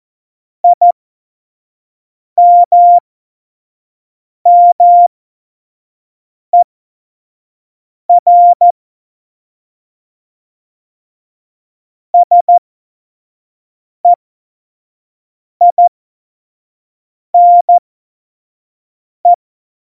Morse code, heard as a keyed tone, about 14 wpm, spaced as for 5 wpm, 700 Hz. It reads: IMMER SEINE
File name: Morsekurs / Witze / Witz 08. Dezember2020, 25 F70